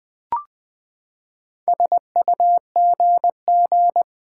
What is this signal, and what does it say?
Morse code, heard as a keyed tone, about 20 wpm, 700 Hz. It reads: SUGG
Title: Build A Word - 4 Letters At A Time - 10-Letter Words - 20wpm